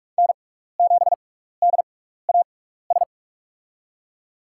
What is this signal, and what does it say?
Morse code, heard as a keyed tone, about 47 wpm, spaced as for 16 wpm, 700 Hz. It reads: N7DAS
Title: USA Call Signs - 5 Characters - 6x Character Spacing - 45wpm